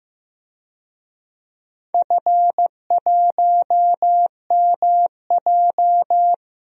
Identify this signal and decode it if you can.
Morse code, heard as a keyed tone, about 15 wpm, 700 Hz. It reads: F1MJ
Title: DXCC-100 Call Signs - 4 Characters - Repeated 2x - 15wpm